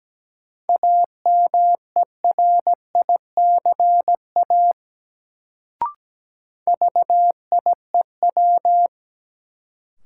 Morse code, heard as a keyed tone, about 17 wpm, 700 Hz. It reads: AMERICA VIEW
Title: Sets of 5 Words - New Word Drill - 17wpm